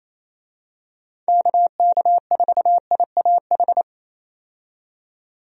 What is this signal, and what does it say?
Morse code, heard as a keyed tone, about 27 wpm, 700 Hz. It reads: KK4IAH
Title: USA Call Signs - 6 Characters - 28wpm